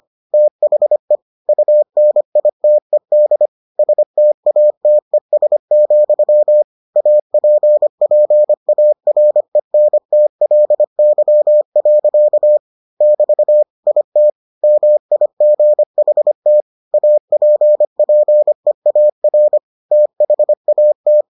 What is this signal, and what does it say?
Morse code, heard as a keyed tone, about 25 wpm, 600 Hz. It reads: THE UNITED STATES, APPARENTLY. <BT> IT MIGHT APPEAR THAT